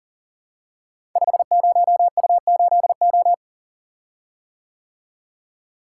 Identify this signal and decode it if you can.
Morse code, heard as a keyed tone, about 38 wpm, 700 Hz. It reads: 50U8O
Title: Code Groups - 5 Letter:Numbers - 40wpm